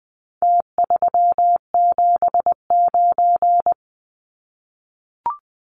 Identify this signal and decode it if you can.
Morse code, heard as a keyed tone, about 20 wpm, 700 Hz. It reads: T379
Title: Numbers - Contest Style - 4-Digit 20wpm